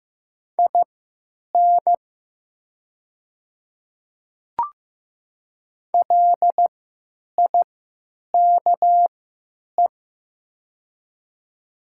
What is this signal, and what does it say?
Morse code, heard as a keyed tone, about 15 wpm, spaced as for 9 wpm, 700 Hz. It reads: IN LIKE
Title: 3x Character Spacing - Top 100 Words - 15wpm